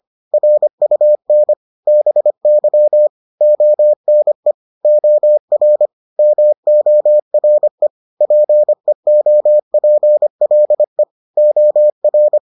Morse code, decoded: RUN BY ONE OR MORE PEOPLE OR

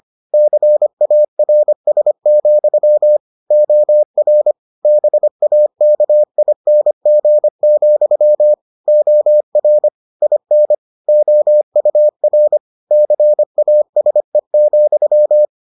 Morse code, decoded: CARS, OR BAKING, OR IN OUR CASE,

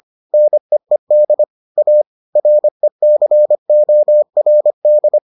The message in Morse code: NEED A RECORD

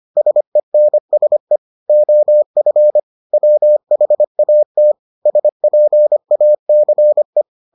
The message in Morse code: SENSE OF WHAT SPACE

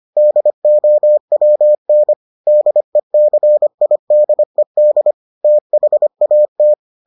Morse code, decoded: DOWN DECIDED THAT

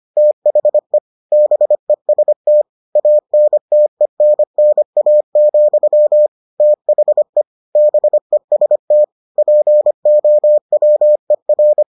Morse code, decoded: THE BEST ANTENNA, THE BEST POWER

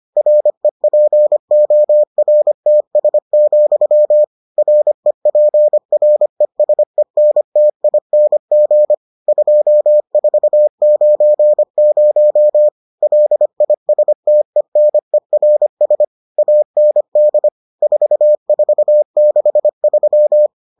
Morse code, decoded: REPORTS, REPRESENTING 2490 LISTENERS AND 4463

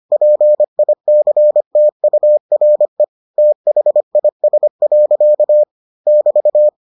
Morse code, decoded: PICTURE THIS. <BT>

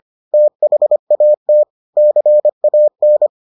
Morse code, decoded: THAT CAN

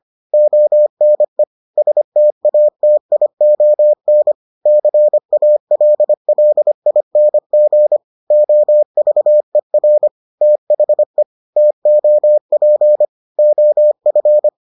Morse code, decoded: ONE STATION CALLING OVER THE TOP OF

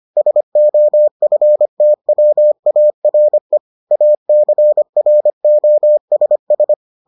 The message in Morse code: SOFTWARE ACROSS

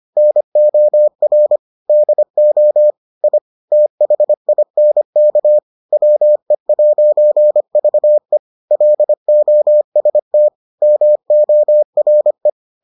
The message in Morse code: NOR DO I THINK WE'VE LOST MORE

